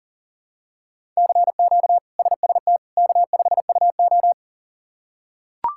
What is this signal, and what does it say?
Morse code, heard as a keyed tone, about 40 wpm, 700 Hz. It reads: CQ SST K5UO
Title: K1USN SST QSO Practice - 40wpm